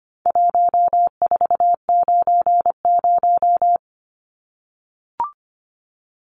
Morse code, heard as a keyed tone, about 25 wpm, 700 Hz. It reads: 1490
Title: Numbers - 4-Digit 25wpm